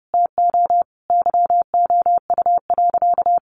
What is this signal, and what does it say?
Morse code, heard as a keyed tone, about 30 wpm, 700 Hz. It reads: TO YOU.
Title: Sentences from Top 1100 Words 30wpm